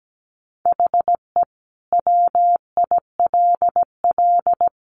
Morse code, heard as a keyed tone, about 17 wpm, 700 Hz. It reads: HE WILL
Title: Sentences from Top 1500 Words 17wpm